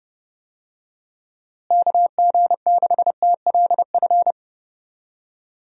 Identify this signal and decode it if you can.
Morse code, heard as a keyed tone, about 32 wpm, 700 Hz. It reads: KG6TLF
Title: USA Call Signs - 6 Characters - Repeated 3x - 30wpm